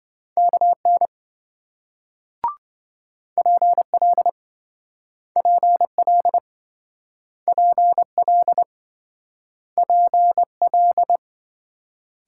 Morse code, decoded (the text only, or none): KN PL PL PL PL